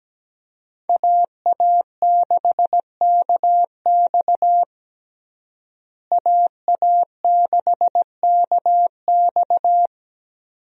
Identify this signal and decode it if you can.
Morse code, heard as a keyed tone, about 17 wpm, 700 Hz. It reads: AA6KX AA6KX
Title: USA Call Signs - 5 Characters - Repeated 2x - 17wpm